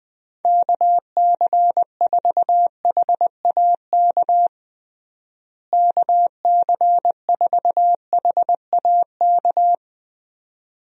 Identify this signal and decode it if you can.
Morse code, heard as a keyed tone, about 20 wpm, 700 Hz. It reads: KC4HAK KC4HAK
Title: USA Call Signs - 6 Characters - Repeated 2x - 20wpm